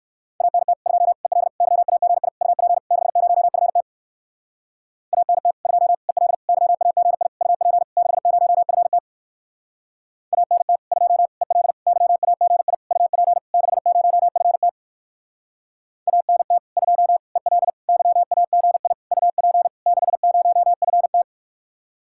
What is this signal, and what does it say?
Morse code, heard as a keyed tone, about 54 wpm, 700 Hz. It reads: ANT 2 EL YAGI UP 60FT ANT 2 EL YAGI UP 60FT ANT 2 EL YAGI UP 60FT ANT 2 EL YAGI UP 60FT
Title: Speed-Racing Common QSO Phrases Encore - 45wpm